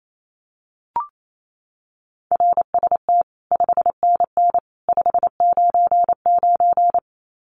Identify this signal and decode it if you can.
Morse code, heard as a keyed tone, about 28 wpm, 700 Hz. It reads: RST 5NN 599
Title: Common QSO Phrases 28wpm